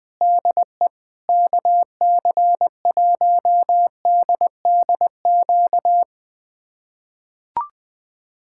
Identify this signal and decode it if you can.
Morse code, heard as a keyed tone, about 20 wpm, 700 Hz. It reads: DE KC1DDQ
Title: USA Callsign DE Callsign - 6 Characters - 20wpm